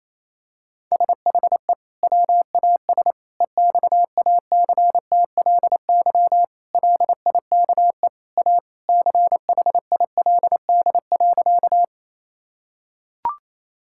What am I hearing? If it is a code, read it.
Morse code, heard as a keyed tone, about 28 wpm, 700 Hz. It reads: SHE WAS EXACTLY LIKE A CHILD.
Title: Sentences from Top 1000 Words - Review Part 6 - 28wpm